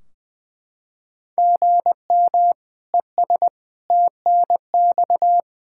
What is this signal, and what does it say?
Morse code, heard as a keyed tone, about 20 wpm, 700 Hz. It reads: GM ES TNX
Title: Practice QSO - Exchange 1 - Beginning - 20wpm